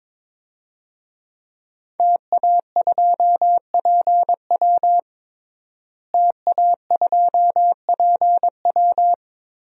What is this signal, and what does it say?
Morse code, heard as a keyed tone, about 22 wpm, 700 Hz. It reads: TA2PW TA2PW
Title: DXCC-100 Call Signs - 5 Characters - Repeated 2x - 22wpm